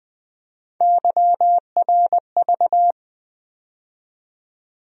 Morse code, heard as a keyed tone, about 21 wpm, 700 Hz. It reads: YRV
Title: USA Call Sign Postfixes - 3 Letters 20wpm